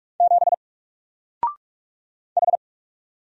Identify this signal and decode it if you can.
Morse code, heard as a keyed tone, about 46 wpm, 700 Hz. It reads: 7 H
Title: Single Letter-Number Alt 45wpm